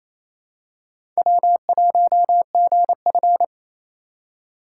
Morse code, heard as a keyed tone, about 28 wpm, 700 Hz. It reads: W1GF
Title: USA Call Signs - 3x Character Spacing - 28wpm